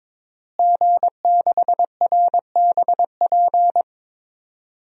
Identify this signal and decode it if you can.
Morse code, heard as a keyed tone, about 22 wpm, 700 Hz. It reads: G6RBP
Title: DXCC-100 Call Signs - 5 Characters - Repeated 2x - 22wpm